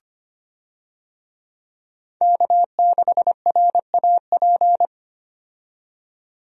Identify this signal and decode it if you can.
Morse code, heard as a keyed tone, about 25 wpm, 700 Hz. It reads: K6RAP
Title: USA Call Signs - 5 Characters - 25wpm